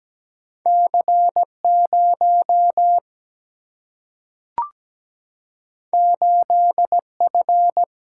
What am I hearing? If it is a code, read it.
Morse code, heard as a keyed tone, about 17 wpm, 700 Hz. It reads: C 0 8 F
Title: Single-Letter Single-Number 17wpm